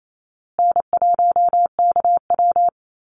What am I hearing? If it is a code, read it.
Morse code, heard as a keyed tone, about 28 wpm, 700 Hz. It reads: N1KW
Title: Call Signs Repeated 2x - 28wpm